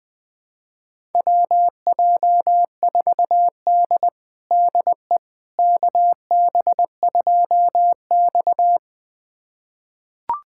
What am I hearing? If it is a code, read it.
Morse code, heard as a keyed tone, about 20 wpm, 700 Hz. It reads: WJ4D DE KB2X
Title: USA Callsign DE Callsign - 4 Characters - 20wpm